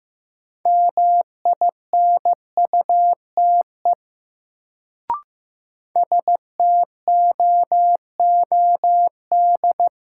Morse code, heard as a keyed tone, about 15 wpm, 700 Hz. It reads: MINUTE STOOD
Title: Sentences from Top 500 Words - New Word Drill - 15wpm